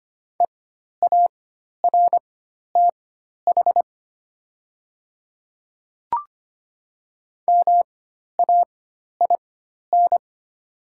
Morse code, heard as a keyed tone, about 25 wpm, spaced as for 12 wpm, 700 Hz. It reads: EARTH MAIN
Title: 4x Character Spacing - Top 300 Words - 25wpm